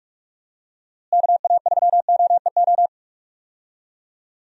Morse code, heard as a keyed tone, about 43 wpm, 700 Hz. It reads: KA3OEO
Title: USA Call Signs - 6 Characters - Repeated 2x - 45wpm